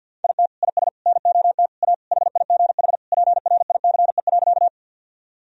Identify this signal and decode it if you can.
Morse code, heard as a keyed tone, about 50 wpm, 700 Hz. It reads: IT IS NOT A HIGH PRICE.
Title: Sentences from Top 1000 Words - Review Part 4 - 50wpm